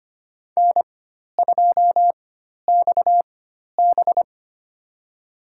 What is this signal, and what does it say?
Morse code, heard as a keyed tone, about 25 wpm, spaced as for 12 wpm, 700 Hz. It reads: N2XB
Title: USA Call Signs - 4 Characters - 4x Character Spacing - 25wpm